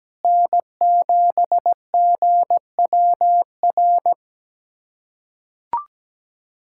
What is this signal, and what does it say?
Morse code, heard as a keyed tone, about 17 wpm, 700 Hz. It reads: N7GWR
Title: USA Callsign DE Callsign - 5 Characters - 17wpm